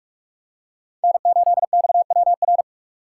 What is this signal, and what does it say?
Morse code, heard as a keyed tone, about 45 wpm, 700 Hz. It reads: N8XWR
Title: USA Call Signs - 5 Characters - 3x Character Spacing - 45wpm